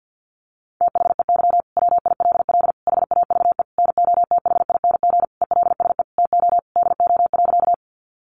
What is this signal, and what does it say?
Morse code, heard as a keyed tone, about 50 wpm, 700 Hz. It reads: THEY WILL HAVE NOTHING ELSE TO DO.
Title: Sentences from Top 800 Words 50wpm